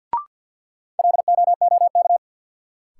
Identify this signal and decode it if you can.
Morse code, heard as a keyed tone, about 48 wpm, 700 Hz. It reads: LOOK
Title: Sentences from Top 100 Words - New Word Drill - 50wpm